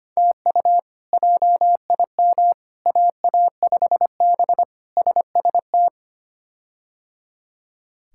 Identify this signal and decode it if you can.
Morse code, heard as a keyed tone, about 25 wpm, 700 Hz. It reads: TU JIM AA5B SST
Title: K1USN SST QSO Practice - 25wpm